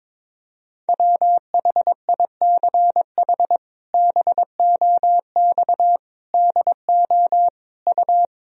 Morse code, heard as a keyed tone, about 22 wpm, 700 Hz. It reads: WHICH BOX DO U
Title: Sentences from Top 500 Words and Abbreviations 22wpm